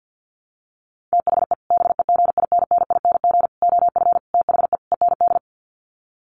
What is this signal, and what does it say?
Morse code, heard as a keyed tone, about 48 wpm, 700 Hz. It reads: THE BEGINNING OF THE END
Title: Sets of 5 Words Encore2 50wpm